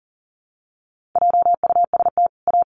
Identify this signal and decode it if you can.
Morse code, heard as a keyed tone, about 40 wpm, 700 Hz. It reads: JUST A
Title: Sets of 2 Words 40wpm